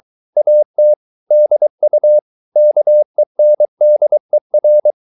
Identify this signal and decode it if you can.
Morse code, heard as a keyed tone, about 23 wpm, 600 Hz. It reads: AT DU KENDER
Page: Ferie 23 wpm